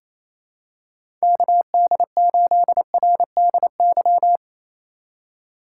Morse code, heard as a keyed tone, about 27 wpm, 700 Hz. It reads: KD8RDY
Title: USA Call Signs - 6 Characters - 4x Character Spacing - 28wpm